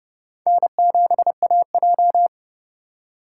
Morse code, heard as a keyed tone, about 30 wpm, 700 Hz. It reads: N7AJ
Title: USA Call Signs - 4 Characters - 6x Character Spacing - 30wpm